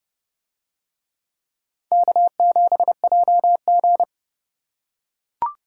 Morse code, heard as a keyed tone, about 29 wpm, 700 Hz. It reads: K7JG